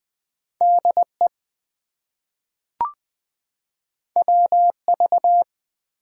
Morse code, heard as a keyed tone, about 20 wpm, 700 Hz. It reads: DE WV
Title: US States and Canadian Provinces 20wpm